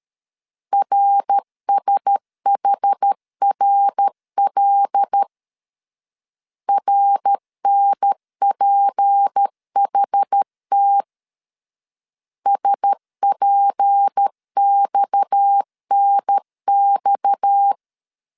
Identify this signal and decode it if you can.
Morse code, heard as a keyed tone, about 13 wpm, 800 Hz. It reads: RSHRL RNPHT SPXNX
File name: LEZIONI / 6 PKX0 / 6 PKX 12wpm 2024